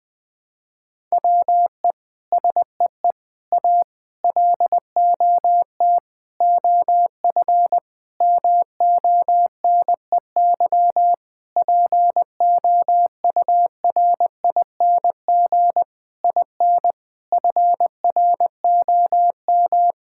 Morse code, heard as a keyed tone, about 20 wpm, 700 Hz. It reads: WE SEE A LOT OF MONEY POURING IN FROM
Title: Sentences with 9-Word Phrases - Spoken Beginning - 20wpm